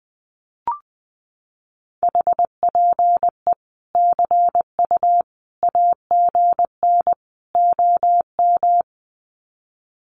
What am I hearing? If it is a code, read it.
Morse code, heard as a keyed tone, about 20 wpm, 700 Hz. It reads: HPE CU AGN OM
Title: Common QSO Phrases 20wpm